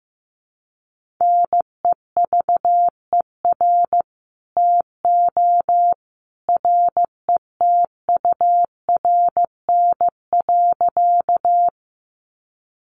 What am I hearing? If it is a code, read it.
Morse code, heard as a keyed tone, about 15 wpm, 700 Hz. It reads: NEVER TO RETURN.
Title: Sentences from Top 800 Words 15wpm